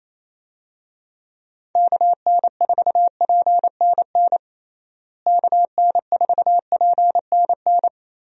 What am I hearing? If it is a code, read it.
Morse code, heard as a keyed tone, about 28 wpm, 700 Hz. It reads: KN4PNN KN4PNN
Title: USA Call Signs - 6 Characters - Repeated 3x - 28wpm